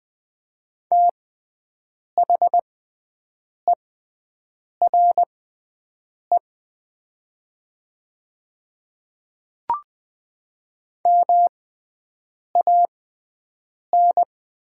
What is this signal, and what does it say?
Morse code, heard as a keyed tone, about 20 wpm, spaced as for 7 wpm, 700 Hz. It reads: THERE MAN
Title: 6x Character Spacing - Top 200 Words - 20wpm